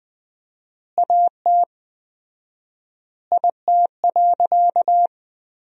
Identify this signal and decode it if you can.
Morse code, heard as a keyed tone, about 20 wpm, 700 Hz. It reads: AT IT.
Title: 4x Word-Spacing - Sentences from Top 100 Words - 20wpm